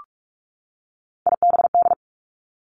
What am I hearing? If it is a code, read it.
Morse code, heard as a keyed tone, about 46 wpm, 700 Hz. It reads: IBD
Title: Three Challenging Letter-Number Combinations 45wpm